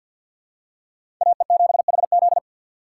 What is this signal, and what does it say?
Morse code, heard as a keyed tone, about 48 wpm, 700 Hz. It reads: AE7HZ
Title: USA Call Signs - 6x Character Spacing - 50wpm